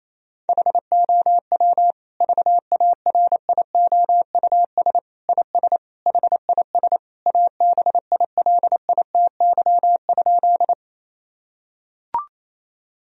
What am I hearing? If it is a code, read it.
Morse code, heard as a keyed tone, about 28 wpm, 700 Hz. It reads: HOW VARIOUS IS HIS ABILITY?
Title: Sentences from Top 1000 Words - Review Part 4 - 28wpm